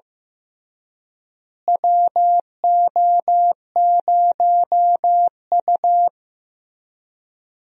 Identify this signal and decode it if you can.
Morse code, heard as a keyed tone, about 15 wpm, 700 Hz. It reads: WO0U
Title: USA Call Signs - 4 Characters - Repeated 3x - 15wpm